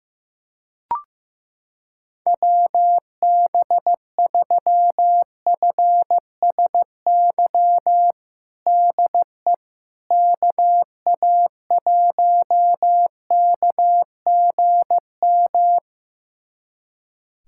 Morse code, heard as a keyed tone, about 15 wpm, 700 Hz. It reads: WB3FSY DE KA1KGM
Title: USA Callsign DE Callsign - 6 Characters - 15wpm